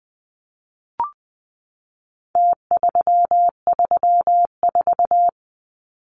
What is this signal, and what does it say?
Morse code, heard as a keyed tone, about 20 wpm, 700 Hz. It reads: T334
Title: Numbers - Contest Style - 4-Digit 20wpm